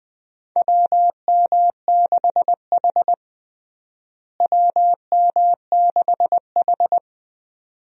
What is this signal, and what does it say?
Morse code, heard as a keyed tone, about 20 wpm, 700 Hz. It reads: WM6H WM6H
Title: USA Call Signs - 4 Characters - Repeated 2x - 20wpm